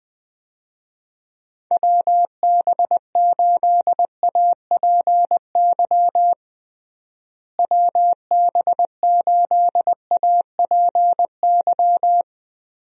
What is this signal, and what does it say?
Morse code, heard as a keyed tone, about 20 wpm, 700 Hz. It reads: WB8APY WB8APY
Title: USA Call Signs - 6 Characters - Repeated 2x - 20wpm